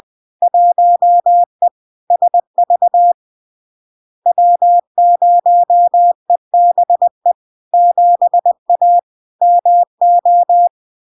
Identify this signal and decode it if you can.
Morse code, heard as a keyed tone, about 20 wpm, 700 Hz. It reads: 1E SV W0EBE 7A MO
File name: LARK-FD-2022-Log-in-CW-20wpm-700-Hz